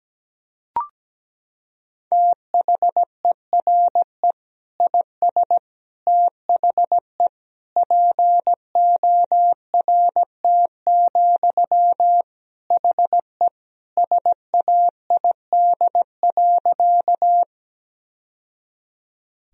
Morse code, heard as a keyed tone, about 17 wpm, 700 Hz. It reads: THERE IS THE PORT, HE SAID.
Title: Sentences from Top 1000 Words - Review Part 1 - 17wpm